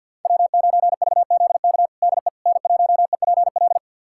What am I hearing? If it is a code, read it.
Morse code, heard as a keyed tone, about 48 wpm, 700 Hz. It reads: W9VZK DE N1EPL
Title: USA Callsign DE Callsign - 5 Characters - 50wpm